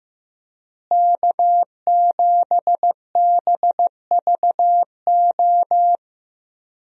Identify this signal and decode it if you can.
Morse code, heard as a keyed tone, about 15 wpm, 700 Hz. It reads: K7BVO
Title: USA Call Signs - 5 Characters - Repeated 2x - 15wpm